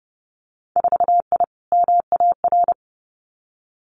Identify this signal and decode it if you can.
Morse code, heard as a keyed tone, about 30 wpm, 700 Hz. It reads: 4I MAR
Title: Winter Field Day Exchanges Encore 30wpm